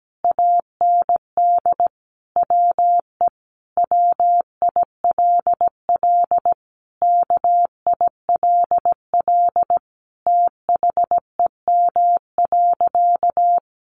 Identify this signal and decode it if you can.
Morse code, heard as a keyed tone, about 17 wpm, 700 Hz. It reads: AND WE WILL KILL THEM.
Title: Sentences from Top 1400 Words 17wpm